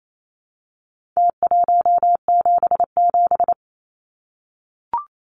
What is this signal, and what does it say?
Morse code, heard as a keyed tone, about 28 wpm, 700 Hz. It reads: T177
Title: Numbers - Contest Style - 4-Digit 28wpm